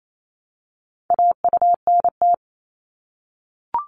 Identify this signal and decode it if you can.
Morse code, heard as a keyed tone, about 28 wpm, 700 Hz. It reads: AUNT
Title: Multi-letter Phonemes Repeated 3x 28wpm